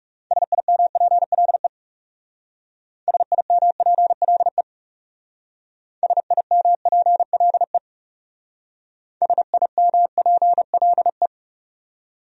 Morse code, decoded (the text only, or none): SIMPLE SIMPLE SIMPLE SIMPLE